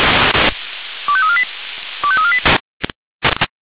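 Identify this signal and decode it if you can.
Five-tone selective call (ZVEI), heard as quick runs of five tones, tones of about 70 ms. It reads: 25348 25348
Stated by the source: ZVEI five-tone selective call received on an SDR.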